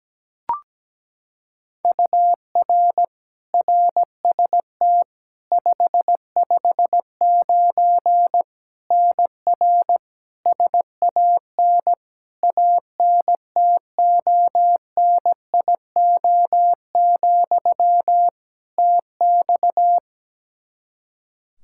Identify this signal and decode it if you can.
Morse code, heard as a keyed tone, about 17 wpm, 700 Hz. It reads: UR RST 559 NR SAN ANTONIO, TX
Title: K1USN SST QSO Practice 17wpm